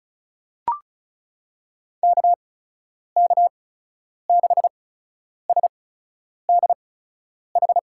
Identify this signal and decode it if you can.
Morse code, heard as a keyed tone, about 36 wpm, spaced as for 10 wpm, 700 Hz. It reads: KK6SDH
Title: USA Call Signs - 6 Characters - 8x Character Spacing - 35wpm